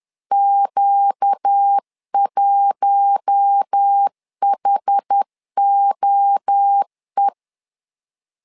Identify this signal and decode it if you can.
Morse code, heard as a keyed tone, about 10 wpm, 800 Hz. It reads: Q1HOE
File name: LEZIONI / 4 Q1 10wpm 2448